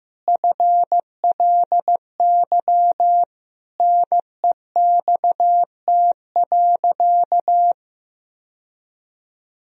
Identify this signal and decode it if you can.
Morse code, heard as a keyed tone, about 16 wpm, 700 Hz. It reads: FLY NEXT.
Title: Sentences from Top 1000 Words - Review Part 2 - 15wpm